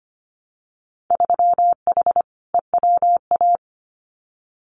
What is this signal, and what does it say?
Morse code, heard as a keyed tone, about 25 wpm, 700 Hz. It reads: 3H EWA
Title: Winter Field Day Exchanges Encore 25wpm